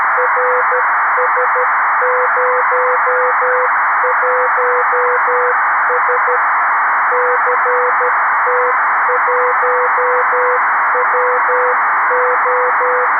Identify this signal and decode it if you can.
Morse code, heard as a keyed tone, about 13 wpm, 500 Hz, with noise. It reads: RS01S CT1WO
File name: arissat / recordings / ARISSat 11Sept2011 1359utc 2mins 14frames